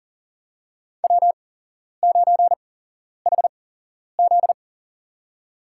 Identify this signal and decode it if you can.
Morse code, heard as a keyed tone, about 40 wpm, spaced as for 11 wpm, 700 Hz. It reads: W9HZ
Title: USA Call Signs - 4 Characters - 8x Character Spacing - 40wpm